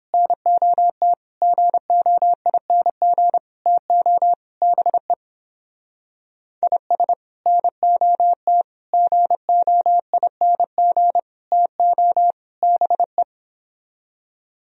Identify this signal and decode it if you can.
Morse code, heard as a keyed tone, about 31 wpm, 700 Hz. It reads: NOT GOING TO BE IS NOT GOING TO BE